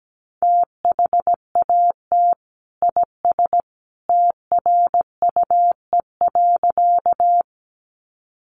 Morse code, heard as a keyed tone, about 17 wpm, 700 Hz. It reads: THAT IS TRUE.
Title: Sentences from Top 400 Words 17wpm